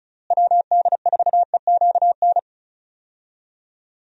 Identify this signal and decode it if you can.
Morse code, heard as a keyed tone, about 36 wpm, 700 Hz. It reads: WD4EQN
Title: USA Call Signs - 6 Characters - Repeated 3x - 35wpm